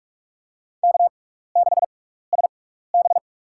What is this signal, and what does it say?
Morse code, heard as a keyed tone, about 47 wpm, spaced as for 16 wpm, 700 Hz. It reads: K6SB
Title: USA Call Signs - 6x Character Spacing - 45wpm